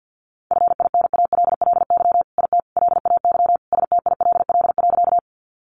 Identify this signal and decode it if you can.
Morse code, heard as a keyed tone, about 50 wpm, 700 Hz. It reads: FINALLY IT LAY STILL.
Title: Sentences from Top 1000 Words 50wpm